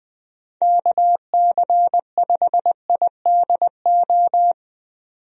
Morse code, heard as a keyed tone, about 20 wpm, 700 Hz. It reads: KC5IDO
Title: USA Call Signs - 6 Characters - 20wpm